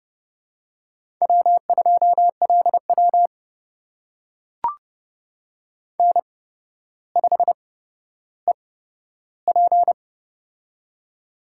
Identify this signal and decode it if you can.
Morse code, heard as a keyed tone, about 30 wpm, spaced as for 13 wpm, 700 Hz. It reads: W2LW N5EP
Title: USA Call Signs - 4 Characters - 8x Character Spacing - 30wpm